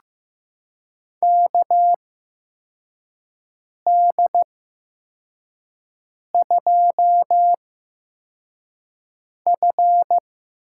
Morse code, heard as a keyed tone, about 15 wpm, spaced as for 4 wpm, 700 Hz. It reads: KD2F
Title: USA Call Signs - 6 Characters - 8x Character Spacing - 15wpm